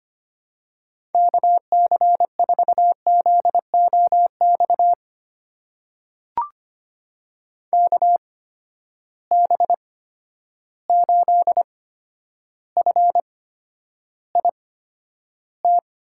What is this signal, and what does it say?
Morse code, heard as a keyed tone, about 25 wpm, spaced as for 11 wpm, 700 Hz. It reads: KC4ZOX KB8FIT
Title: USA Call Signs - 6 Characters - 8x Character Spacing - 25wpm